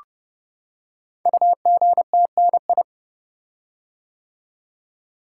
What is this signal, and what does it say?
Morse code, heard as a keyed tone, about 28 wpm, 700 Hz. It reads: UGTNI
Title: Code Groups - 5 Letters - 30wpm